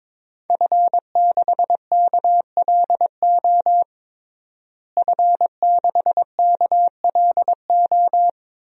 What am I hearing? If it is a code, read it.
Morse code, heard as a keyed tone, about 21 wpm, 700 Hz. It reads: F6KLO F6KLO
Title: DXCC-100 Call Signs - 5 Characters - Repeated 3x - 22wpm